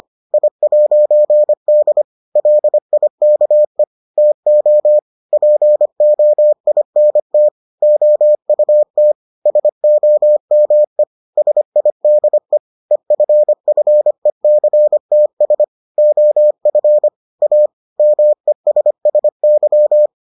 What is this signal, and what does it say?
Morse code, heard as a keyed tone, about 25 wpm, 600 Hz. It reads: I'D LIKE TO POINT OUT SOME SIDE EFFECTS OF A MESSY